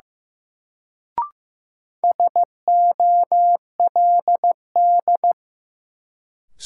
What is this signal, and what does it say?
Morse code, heard as a keyed tone, about 15 wpm, 700 Hz. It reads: SOLD
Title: Sentences from Top 1400 Words - New Word Drill Alt - 15wpm